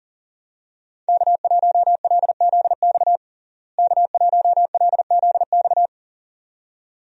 Morse code, heard as a keyed tone, about 40 wpm, 700 Hz. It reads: K1LZX K1LZX
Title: USA Call Signs - 5 Characters - Repeated 2x - 40wpm